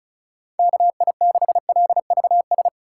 Morse code, heard as a keyed tone, about 36 wpm, 700 Hz. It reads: KI6LVS
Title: USA Call Signs - 6 Characters - 4x Character Spacing - 35wpm